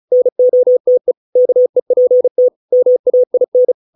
Morse code, heard as a keyed tone, about 34 wpm, 500 Hz. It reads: NOTE KEPT MAIN